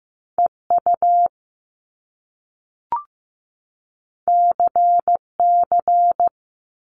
Morse code, heard as a keyed tone, about 15 wpm, 700 Hz. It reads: EU CC
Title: Top Two Letter Combinations in Words 15wpm